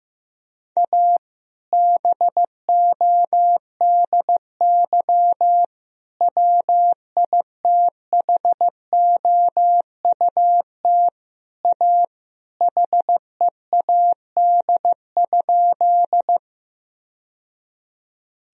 Morse code, heard as a keyed tone, about 15 wpm, 700 Hz. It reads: A BODY WITHOUT A HEAD?
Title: Sentences from Top 1000 Words - Review Part 3 - 15wpm